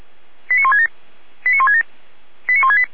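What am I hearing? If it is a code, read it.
Five-tone selective call (ZVEI), heard as quick runs of five tones, tones of about 70 ms. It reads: 89167 89167 89168
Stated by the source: ZVEI five-tone call received on SDR, fire brigade Feuerwehr Orlamünde.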